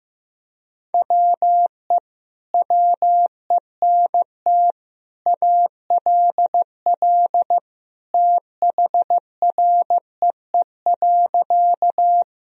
Morse code, heard as a keyed tone, about 15 wpm, 700 Hz. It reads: WE WENT ALL THREE.
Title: Sentences from Top 1000 Words - Review Part 1 - 15wpm